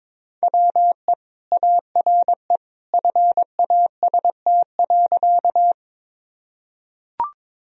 Morse code, heard as a keyed tone, about 22 wpm, 700 Hz. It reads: WE ARE FAST.
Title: Sentences from Top 1000 Words - Review Part 2 - 22wpm